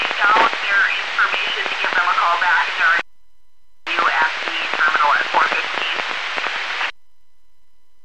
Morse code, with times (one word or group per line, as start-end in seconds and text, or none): none